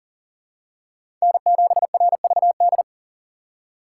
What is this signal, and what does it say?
Morse code, heard as a keyed tone, about 40 wpm, 700 Hz. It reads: N7RVD
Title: K1USN SST QSO Practice - 40wpm